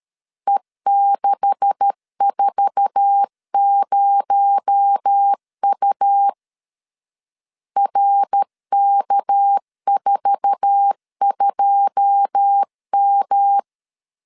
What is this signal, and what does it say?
Morse code, heard as a keyed tone, about 13 wpm, 800 Hz. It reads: E640U RK42M
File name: LEZIONI / 10 SIGNS / 10A SLASH 12wpm 2314